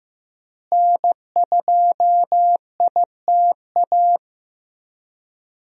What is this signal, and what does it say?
Morse code, heard as a keyed tone, about 15 wpm, 700 Hz. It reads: N2ITA
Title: USA Call Signs - 5 Characters - Repeated 3x - 15wpm